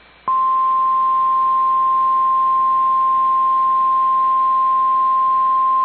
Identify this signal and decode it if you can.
Emergency Alert System (SAME) tones (the 1050 Hz attention tone). It is heard from 0.2 to 4.8 s.